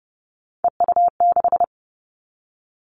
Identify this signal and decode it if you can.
Morse code, heard as a keyed tone, about 30 wpm, 700 Hz. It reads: EU6
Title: Callsign Prefixes for Top 100 DXCC - 30wpm